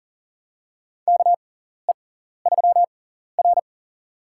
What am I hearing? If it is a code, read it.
Morse code, heard as a keyed tone, about 38 wpm, spaced as for 14 wpm, 700 Hz. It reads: KE3R
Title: USA Call Signs - 4 Characters - 6x Character Spacing - 40wpm